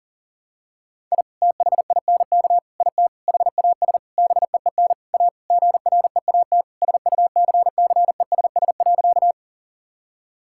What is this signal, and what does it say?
Morse code, heard as a keyed tone, about 40 wpm, 700 Hz. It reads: I THINK IT HAS BEEN A GREAT SUCCESS.
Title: Sentences from Top 1000 Words - Review Part 6 - 40wpm